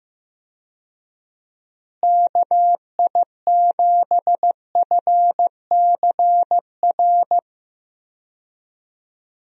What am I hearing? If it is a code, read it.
Morse code, heard as a keyed tone, about 15 wpm, 700 Hz. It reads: KI7FCR
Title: USA Call Signs - 6 Characters - Repeated 2x - 15wpm